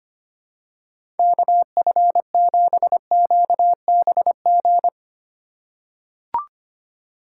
Morse code, heard as a keyed tone, about 25 wpm, 700 Hz. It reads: KF7QBG
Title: USA Call Signs - 6 Characters - 2x Character Spacing - 25wpm